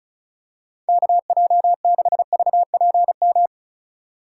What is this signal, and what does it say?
Morse code, heard as a keyed tone, about 34 wpm, 700 Hz. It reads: KJ6VPM
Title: USA Call Signs - 6 Characters - 3x Character Spacing - 35wpm